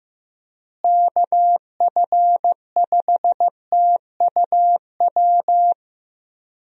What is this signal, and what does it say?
Morse code, heard as a keyed tone, about 15 wpm, 700 Hz. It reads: KF5TUW
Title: USA Call Signs - 6 Characters - Repeated 3x - 15wpm